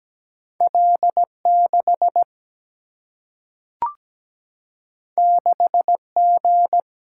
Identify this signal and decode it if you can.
Morse code, heard as a keyed tone, about 17 wpm, 700 Hz. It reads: L 6 6 G
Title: Single-Letter Single-Number 17wpm